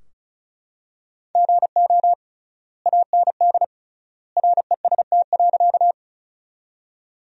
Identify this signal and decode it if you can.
Morse code, heard as a keyed tone, about 34 wpm, 700 Hz. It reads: GO AND REST.
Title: 3x Word-Spacing - Sentences from Top 500 Words - 35wpm